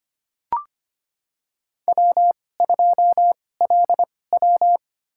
Morse code, heard as a keyed tone, about 25 wpm, spaced as for 18 wpm, 700 Hz. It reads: W2LW
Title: USA Call Signs - 4 Characters - 2x Character Spacing - 25wpm